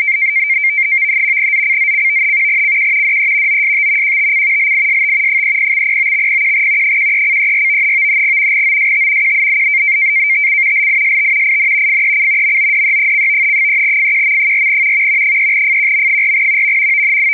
NAVTEX (SITOR-B), about 100 baud. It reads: ZCZC CQ DE OST QTC LIST 30/09/01 13:34 OST QTC LIST IN FEC MODE: 3FQM4 3FQM4
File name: uploader / sitor b 100baud fec